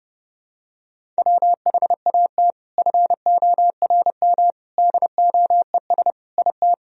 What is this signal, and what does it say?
Morse code, heard as a keyed tone, about 30 wpm, 700 Hz. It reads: WHAT FORM DOES IT
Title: Sentences from Top 1000 Words - Review Part 1 - 30wpm